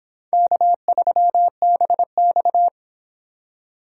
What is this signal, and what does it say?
Morse code, heard as a keyed tone, about 26 wpm, 700 Hz. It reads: K3BX